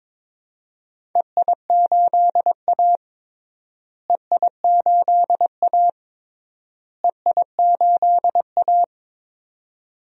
Morse code, heard as a keyed tone, about 22 wpm, 700 Hz. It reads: EI8A EI8A EI8A
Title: DXCC-100 Call Signs - 4 Characters - Repeated 3x - 22wpm